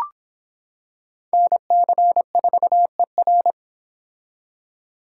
Morse code, heard as a keyed tone, about 26 wpm, 700 Hz. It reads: NC4ER